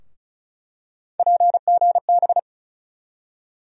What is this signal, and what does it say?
Morse code, heard as a keyed tone, about 34 wpm, 700 Hz. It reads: PGB
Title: USA Call Sign Postfixes - 3 Letters 35wpm